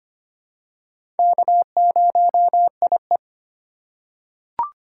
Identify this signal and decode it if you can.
Morse code, heard as a keyed tone, about 25 wpm, 700 Hz. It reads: K0IE
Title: USA Call Signs - 4 Characters - 3x Character Spacing - 25wpm